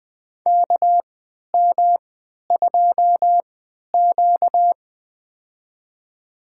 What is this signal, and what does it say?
Morse code, heard as a keyed tone, about 20 wpm, spaced as for 11 wpm, 700 Hz. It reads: KM2Q
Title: USA Call Signs - 4 Characters - 3x Character Spacing - 20wpm